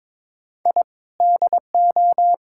Morse code, heard as a keyed tone, about 22 wpm, 700 Hz. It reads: I DO
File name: practice / 22wpm / Build A Phrase - 5-Word Phrases - 22wpm